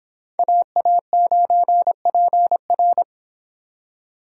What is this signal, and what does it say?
Morse code, heard as a keyed tone, about 26 wpm, 700 Hz. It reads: AA9PR